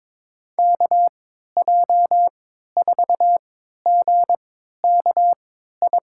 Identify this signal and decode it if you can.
Morse code, heard as a keyed tone, about 22 wpm, spaced as for 13 wpm, 700 Hz. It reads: KJ4GKI
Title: USA Call Signs - 6 Characters - 3x Character Spacing - 22wpm